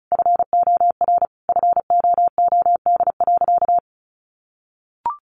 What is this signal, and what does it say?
Morse code, heard as a keyed tone, about 34 wpm, 700 Hz. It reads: FOR FOOD.
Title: A Sound Of Thunder 35wpm